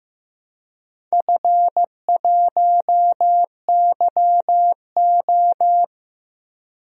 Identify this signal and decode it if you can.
Morse code, heard as a keyed tone, about 15 wpm, 700 Hz. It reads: F1YO
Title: DXCC-100 Call Signs - 4 Characters - 15wpm